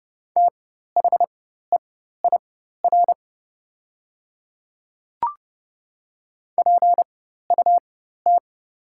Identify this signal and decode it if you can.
Morse code, heard as a keyed tone, about 31 wpm, spaced as for 14 wpm, 700 Hz. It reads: THEIR PUT
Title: 4x Character Spacing - Top 200 Words - 30wpm